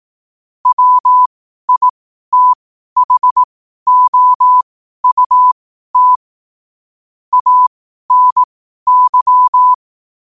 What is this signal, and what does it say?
Morse code, heard as a keyed tone, about 18 wpm, spaced as for 12 wpm, 1000 Hz. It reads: WITHOUT ANY
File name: qotdfiles / 12wpm